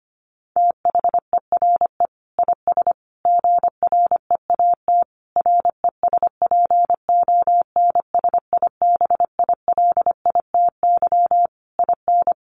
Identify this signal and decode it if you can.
Morse code, heard as a keyed tone, about 25 wpm, 700 Hz. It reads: THERE IS GREAT RESPONSIBILITY IN
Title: Sentences from Top 1200 Words 25wpm